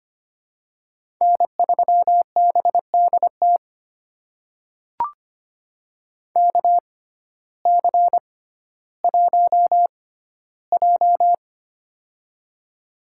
Morse code, heard as a keyed tone, about 25 wpm, spaced as for 14 wpm, 700 Hz. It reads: N3BDT KC1J
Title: USA Call Signs - 6x Character Spacing - 25wpm